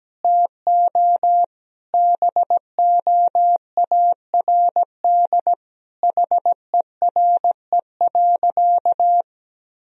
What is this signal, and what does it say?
Morse code, heard as a keyed tone, about 17 wpm, 700 Hz. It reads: TO BOARD HERE.
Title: Sentences from Top 1000 Words - Review Part 4 - 17wpm